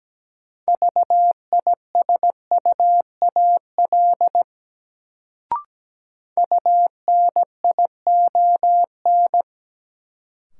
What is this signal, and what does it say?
Morse code, heard as a keyed tone, about 17 wpm, 700 Hz. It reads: VISUAL UNION
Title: Sets of 5 Words Encore - New Word Drill - 17wpm